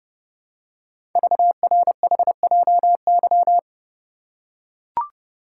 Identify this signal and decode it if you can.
Morse code, heard as a keyed tone, about 29 wpm, 700 Hz. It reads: VRHJY
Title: Code Groups - 5 Letter:Numbers - 30wpm